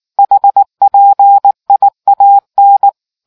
Morse code, heard as a keyed tone, about 19 wpm, 800 Hz. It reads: HPIAN